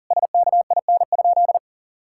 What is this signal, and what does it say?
Morse code, heard as a keyed tone, about 37 wpm, 700 Hz. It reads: SKIN?
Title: Sentences from Top 1600 Words 40wpm